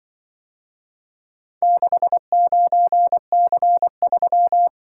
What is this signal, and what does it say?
Morse code, heard as a keyed tone, about 23 wpm, 700 Hz. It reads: 69C3
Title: Speed-Racing - Four Letter-Number Combinations - 20wpm